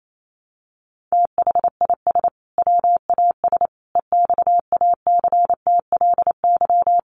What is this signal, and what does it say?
Morse code, heard as a keyed tone, about 28 wpm, 700 Hz. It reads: THIS WAS EXACTLY
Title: Sentences from Top 900 Words 28wpm